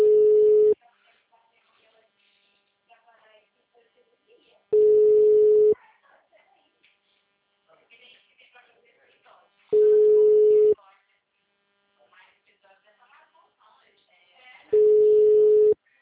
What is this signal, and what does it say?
Ringback tone (Europe, 425 Hz), with speech.